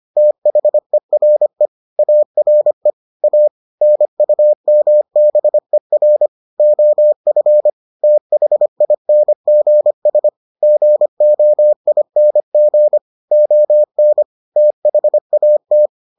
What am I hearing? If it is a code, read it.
Morse code, heard as a keyed tone, about 25 wpm, 600 Hz. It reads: THERE ARE A NUMBER OF THINGS GOING ON THAT